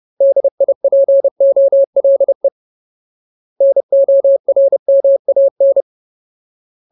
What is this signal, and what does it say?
Morse code, heard as a keyed tone, about 30 wpm, 550 Hz. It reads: DIPOLE NORMAN
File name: Ultimate Word List / 30WPM Ultimate